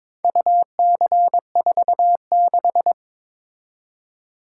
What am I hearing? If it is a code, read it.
Morse code, heard as a keyed tone, about 22 wpm, 700 Hz. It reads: UC46
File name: practice / 22wpm / Four Letter-Number Combinations - 22wpm